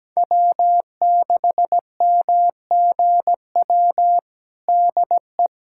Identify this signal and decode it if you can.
Morse code, heard as a keyed tone, about 17 wpm, 700 Hz. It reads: W6MGW DE
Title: USA Callsign DE Callsign 17wpm